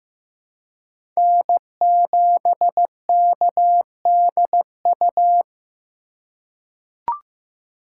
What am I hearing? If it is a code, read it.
Morse code, heard as a keyed tone, about 15 wpm, 700 Hz. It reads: N7KDU
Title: USA Call Signs - 5 Characters - 4x Character Spacing - 15wpm